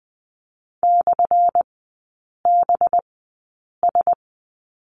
Morse code, heard as a keyed tone, about 20 wpm, spaced as for 8 wpm, 700 Hz. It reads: /BS
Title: 184 Lesson 30e - Letter b - Three Characters 20wpm